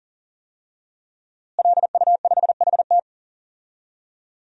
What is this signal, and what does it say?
Morse code, heard as a keyed tone, about 40 wpm, 700 Hz. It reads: LU5HT
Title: DXCC-100 Call Signs - 5 Characters - Repeated 2x - 40wpm